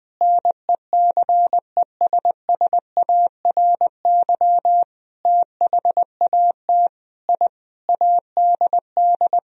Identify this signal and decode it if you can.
Morse code, heard as a keyed tone, about 20 wpm, 700 Hz. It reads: NECESSARY THAT I ADD
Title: Sentences from Top 1000 Words - Review Part 1 - 20wpm